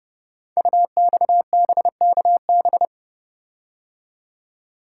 Morse code, heard as a keyed tone, about 30 wpm, 700 Hz. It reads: UXBKB
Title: Code Groups - 5 Letters - 30wpm